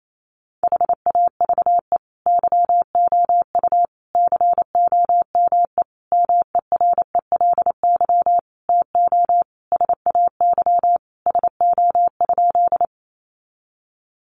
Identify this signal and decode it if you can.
Morse code, heard as a keyed tone, about 28 wpm, 700 Hz. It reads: HAVE YOU COME MERELY TO SAY SO?
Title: Sentences from Top 1400 Words 28wpm